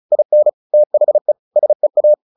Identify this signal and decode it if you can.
Morse code, heard as a keyed tone, about 34 wpm, 600 Hz. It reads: IN THE SEA
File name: the old man and the sea 2 35wpm